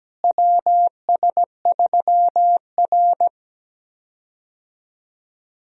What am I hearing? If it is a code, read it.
Morse code, heard as a keyed tone, about 17 wpm, 700 Hz. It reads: WS3R
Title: USA Call Signs - 4 Characters - Repeated 3x - 17wpm